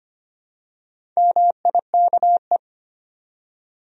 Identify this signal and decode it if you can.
Morse code, heard as a keyed tone, about 25 wpm, 700 Hz. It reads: MIKE
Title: USA Common Names - Max 5 Letters - 25wpm